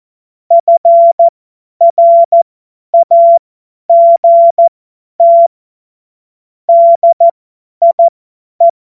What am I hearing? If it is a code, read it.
Morse code, heard as a keyed tone, about 14 wpm, spaced as for 10 wpm, 680 Hz. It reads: FRAGT DIE
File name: Morsekurs / Witze / Witz 07. März 2023 50 F70